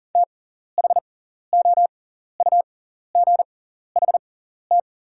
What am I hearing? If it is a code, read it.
Morse code, heard as a keyed tone, about 40 wpm, spaced as for 14 wpm, 690 Hz. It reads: THOUGHT
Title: 6x Character Spacing - Top 300 Words - 40wpm